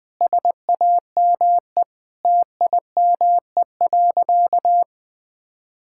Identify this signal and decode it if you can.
Morse code, heard as a keyed tone, about 20 wpm, 700 Hz. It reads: SAME TIME.
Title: Sentences with 8-Word Phrases - 20wpm